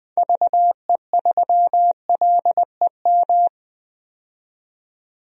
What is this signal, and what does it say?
Morse code, heard as a keyed tone, about 20 wpm, 700 Hz. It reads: VE3LEM
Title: DXCC-100 Call Signs - 6 Characters - 20wpm